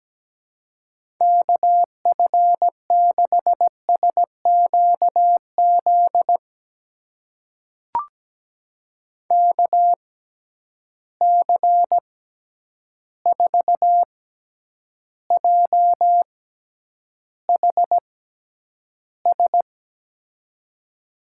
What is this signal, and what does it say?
Morse code, heard as a keyed tone, about 17 wpm, spaced as for 9 wpm, 700 Hz. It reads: KF6SQZ KC4JHS
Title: USA Call Signs - 6 Characters - 6x Character Spacing - 17wpm